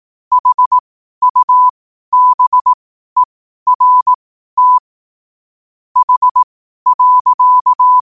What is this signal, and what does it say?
Morse code, heard as a keyed tone, about 18 wpm, spaced as for 12 wpm, 1000 Hz. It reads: HUBERT H.
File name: qotdfiles / 12wpm